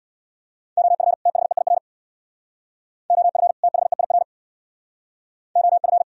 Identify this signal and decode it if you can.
Morse code, heard as a keyed tone, about 64 wpm, 700 Hz. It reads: OF THEIR OF THEIR OF